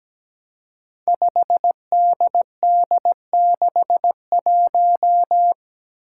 Morse code, heard as a keyed tone, about 17 wpm, 700 Hz. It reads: 5DD61
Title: Five Challenging Letter-Number Combinations 17wpm